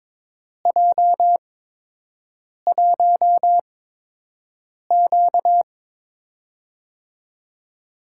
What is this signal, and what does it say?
Morse code, heard as a keyed tone, about 22 wpm, spaced as for 6 wpm, 700 Hz. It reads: J1Q
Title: USA Call Signs - 8x Character Spacing - 22wpm